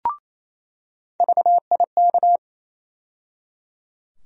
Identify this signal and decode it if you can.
Morse code, heard as a keyed tone, about 28 wpm, 700 Hz. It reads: VIK
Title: USA Call Sign Postfixes - 3 Letters 28wpm